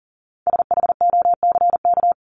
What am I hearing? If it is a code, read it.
Morse code, heard as a keyed tone, about 40 wpm, 700 Hz. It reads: SHOCK
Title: Top 1751-2000 Words 40wpm